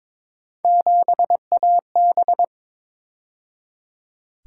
Morse code, heard as a keyed tone, about 27 wpm, 700 Hz. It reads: 7AB
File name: Speed-Racing - Three Letter-Number Combinations - 22wpm